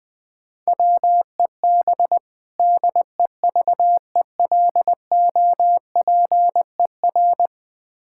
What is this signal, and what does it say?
Morse code, heard as a keyed tone, about 21 wpm, 700 Hz. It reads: WEB DEVELOPER
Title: Practice QSO - Exchange 3 - More Info - 20wpm